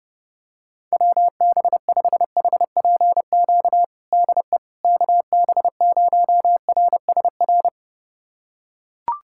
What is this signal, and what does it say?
Morse code, heard as a keyed tone, about 30 wpm, 700 Hz. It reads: WB5HPQ DE KB0RSR
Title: USA Callsign DE Callsign - 6 Characters - 30wpm